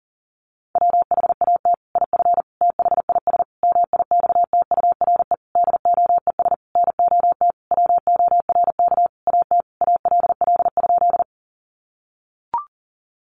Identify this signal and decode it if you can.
Morse code, heard as a keyed tone, about 40 wpm, 700 Hz. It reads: WHAT IF THIS MIXTURE DOES NOT WORK AT ALL?
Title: Sentences from Top 1500 Words 40wpm